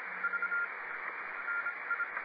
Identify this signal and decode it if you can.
Morse code, heard as a keyed tone, about 26 wpm, spaced as for 22 wpm, 1400 Hz, with radio noise. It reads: V NI